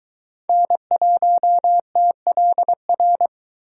Morse code, heard as a keyed tone, about 23 wpm, 700 Hz. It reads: N1TLR